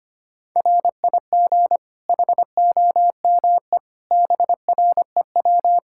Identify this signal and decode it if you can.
Morse code, heard as a keyed tone, about 25 wpm, 700 Hz. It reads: RIG HOME BREW
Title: K1USN SST QSO Practice 25wpm